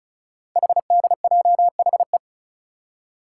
Morse code, heard as a keyed tone, about 36 wpm, 690 Hz. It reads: HDJHE
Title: Code Groups - 5 Letters - 35wpm